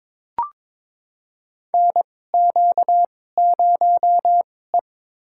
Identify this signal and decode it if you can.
Morse code, heard as a keyed tone, about 22 wpm, spaced as for 16 wpm, 700 Hz. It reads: NQ0E
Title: USA Call Signs - 4 Characters - 2x Character Spacing - 22wpm